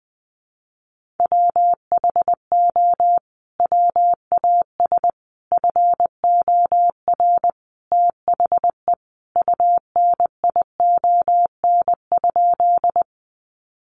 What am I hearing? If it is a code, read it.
Morse code, heard as a keyed tone, about 20 wpm, 700 Hz. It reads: WHO WAS FOR THE UNION?
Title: Sentences from Top 1400 Words 20wpm